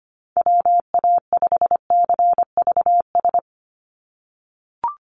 Morse code, heard as a keyed tone, about 25 wpm, 700 Hz. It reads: WA5CVS
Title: USA Call Signs 25wpm